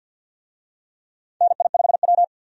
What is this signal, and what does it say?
Morse code, heard as a keyed tone, about 53 wpm, 700 Hz. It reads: NI5W
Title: USA Call Signs - 4 Characters - 50wpm